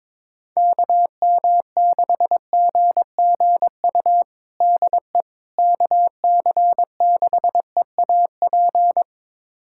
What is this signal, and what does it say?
Morse code, heard as a keyed tone, about 22 wpm, 700 Hz. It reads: KM6GGU DE KC6EAP
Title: USA Callsign DE Callsign 22wpm